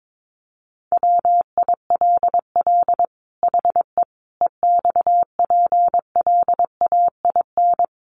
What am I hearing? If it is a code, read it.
Morse code, heard as a keyed tone, about 22 wpm, 700 Hz. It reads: WILL HE EXPLAIN
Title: Sentences from Top 1100 Words 22wpm